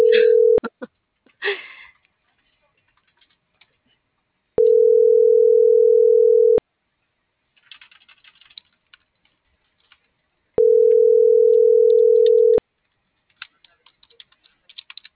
Ringback tone (North America), 440 + 480 Hz.